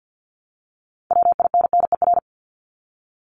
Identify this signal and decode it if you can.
Morse code, heard as a keyed tone, about 50 wpm, 700 Hz. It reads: WINNER
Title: Top 1751-2000 Words 50wpm